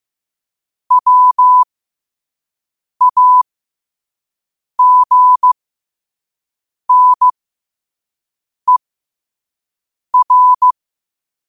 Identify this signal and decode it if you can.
Morse code, heard as a keyed tone, about 15 wpm, spaced as for 5 wpm, 1000 Hz. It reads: WAGNER
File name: qotdfiles / 5wpm